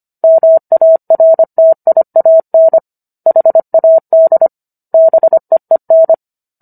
Morse code, heard as a keyed tone, about 25 wpm, 650 Hz. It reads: MARTIAN HAD BEEN